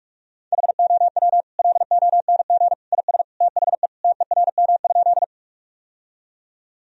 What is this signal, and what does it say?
Morse code, heard as a keyed tone, about 42 wpm, 700 Hz. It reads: HOW LONG IS THE TERM?
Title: Sentences from Top 1000 Words - Review Part 4 - 45wpm